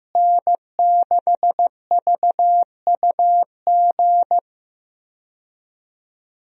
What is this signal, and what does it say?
Morse code, heard as a keyed tone, about 15 wpm, 700 Hz. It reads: N6VUG
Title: USA Call Signs - 5 Characters - Repeated 2x - 15wpm